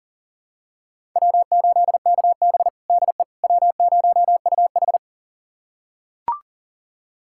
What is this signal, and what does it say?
Morse code, heard as a keyed tone, about 38 wpm, 700 Hz. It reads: W8KB DE W0UH
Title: USA Callsign DE Callsign - 4 Characters - 40wpm